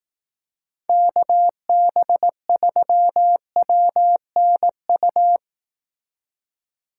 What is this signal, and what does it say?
Morse code, heard as a keyed tone, about 18 wpm, 700 Hz. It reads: KB3WNU
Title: Speed Racing - USA 2x3 Call Signs 15wpm